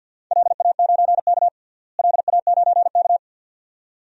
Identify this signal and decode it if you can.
Morse code, heard as a keyed tone, about 48 wpm, 700 Hz. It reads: LA9K LA9K
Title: DXCC-100 Call Signs - 4 Characters - Repeated 2x - 50wpm